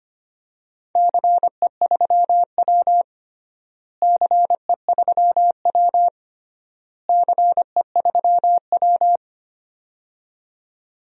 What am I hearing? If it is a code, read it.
Morse code, heard as a keyed tone, about 25 wpm, 700 Hz. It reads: CE3W CE3W CE3W
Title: DXCC-100 Call Signs - 4 Characters - Repeated 3x - 25wpm